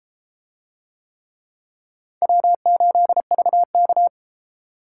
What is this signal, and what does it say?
Morse code, heard as a keyed tone, about 29 wpm, 700 Hz. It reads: W8VK